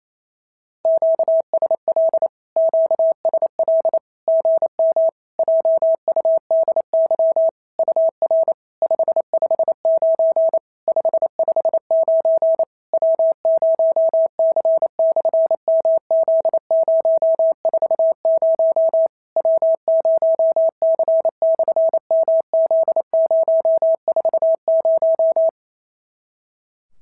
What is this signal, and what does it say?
Morse code, heard as a keyed tone, about 28 wpm, 650 Hz. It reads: QSL QSL GM JUDY UR 559 559 W0C/MZ040 W0C/MZ040
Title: SOTA QSO Practice 28wpm